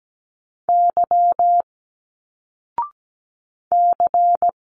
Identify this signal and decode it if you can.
Morse code, heard as a keyed tone, about 17 wpm, 700 Hz. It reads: Y C
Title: Single Letters 17wpm